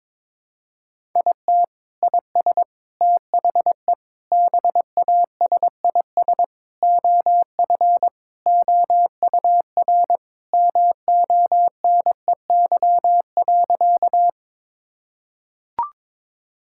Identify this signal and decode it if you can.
Morse code, heard as a keyed tone, about 21 wpm, 700 Hz. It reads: IT IS THE BASIS OF OUR MONEY.
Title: Sentences from Top 1000 Words - Review Part 5 - 22wpm